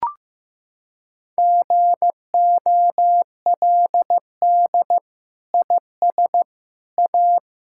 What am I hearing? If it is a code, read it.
Morse code, heard as a keyed tone, about 15 wpm, 700 Hz. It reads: GOLD IS A
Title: Sentences from Top 1000 Words - Review Part 6 - 15wpm